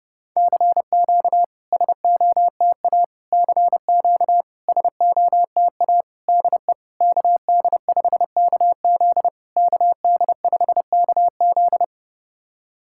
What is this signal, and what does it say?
Morse code, heard as a keyed tone, about 30 wpm, 690 Hz. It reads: CQ SOTA CQ SOTA DE KD5KZ KD5KZ
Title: SOTA QSO Practice 30wpm